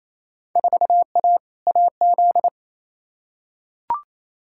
Morse code, heard as a keyed tone, about 28 wpm, 700 Hz. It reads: 4A AZ
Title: Field Day Exchanges Encore - 28wpm